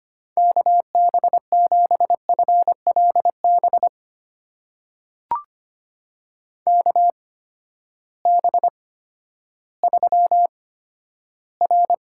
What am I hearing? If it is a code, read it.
Morse code, heard as a keyed tone, about 25 wpm, spaced as for 12 wpm, 700 Hz. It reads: KB7FLB KB3R
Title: USA Call Signs - 6 Characters - 8x Character Spacing - 25wpm